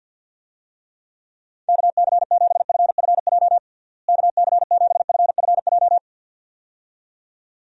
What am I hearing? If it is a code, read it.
Morse code, heard as a keyed tone, about 52 wpm, 700 Hz. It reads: KC7FFJ KC7FFJ
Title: USA Call Signs - 6 Characters - Repeated 2x - 50wpm